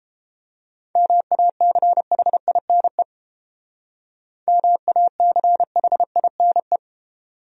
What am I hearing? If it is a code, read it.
Morse code, heard as a keyed tone, about 29 wpm, 700 Hz. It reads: MACHINE MACHINE